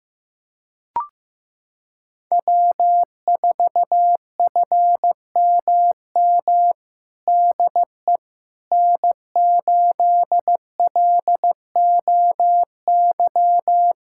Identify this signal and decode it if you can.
Morse code, heard as a keyed tone, about 15 wpm, 700 Hz. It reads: W4FMM DE N8LOY
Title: USA Callsign DE Callsign - 5 Characters - 15wpm